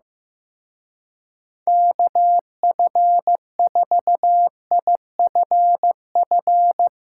Morse code, heard as a keyed tone, about 16 wpm, 700 Hz. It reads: KF4IFF
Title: Speed Racing - USA Call Signs 15wpm